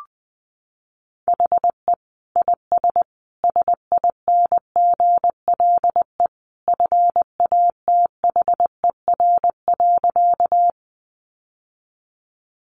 Morse code, heard as a keyed tone, about 20 wpm, 700 Hz. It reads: HE IS SINGLE FATHER.
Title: Sentences from Top 700 Words 20wpm